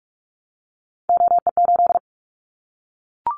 Morse code, heard as a keyed tone, about 44 wpm, 700 Hz. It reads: OE8
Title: Callsign Prefixes for Top 100 DXCC - 45wpm